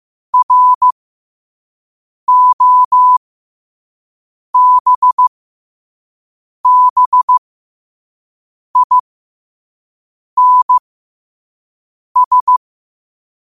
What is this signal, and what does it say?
Morse code, heard as a keyed tone, about 15 wpm, spaced as for 5 wpm, 1000 Hz. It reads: ROBBINS
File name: qotdfiles / 5wpm